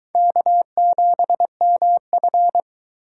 Morse code, heard as a keyed tone, about 20 wpm, 700 Hz. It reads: K7MF